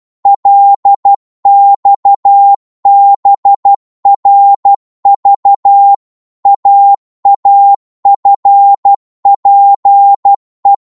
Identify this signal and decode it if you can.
Morse code, heard as a keyed tone, about 12 wpm, 800 Hz. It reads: LXBRV AAFPE
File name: Morse / ltr-12wpm